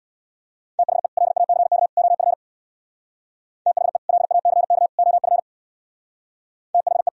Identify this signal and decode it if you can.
Morse code, heard as a keyed tone, about 62 wpm, 700 Hz. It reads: THE LACK OF THE LACK OF THE
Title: Speed-Racing - Sets of 3 Words 50wpm